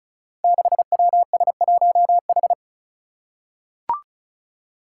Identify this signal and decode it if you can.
Morse code, heard as a keyed tone, about 36 wpm, 700 Hz. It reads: 6WS1H
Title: Five Challenging Letter-Number Combinations 35wpm